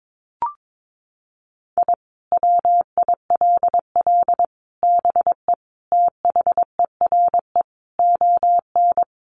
Morse code, heard as a keyed tone, about 22 wpm, 700 Hz. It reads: I WILL BE THERE ON
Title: Sentences from Top 1500 Words 22wpm